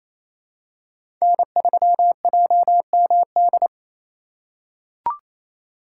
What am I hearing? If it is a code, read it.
Morse code, heard as a keyed tone, about 27 wpm, 700 Hz. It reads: N3JMD
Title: USA Call Signs - 5 Characters - 2x Character Spacing - 28wpm